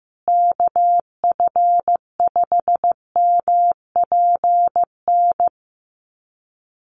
Morse code, heard as a keyed tone, about 15 wpm, 700 Hz. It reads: KF5MPN
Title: USA Call Signs Repeated 2x 15wpm